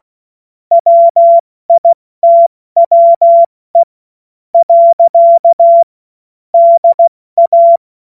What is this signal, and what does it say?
Morse code, heard as a keyed tone, about 16 wpm, spaced as for 13 wpm, 680 Hz. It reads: WITWE. DA